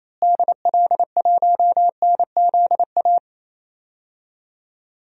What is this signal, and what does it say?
Morse code, heard as a keyed tone, about 28 wpm, 700 Hz. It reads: DL1NZA
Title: DXCC-100 Call Signs - 6 Characters - Repeated 3x - 28wpm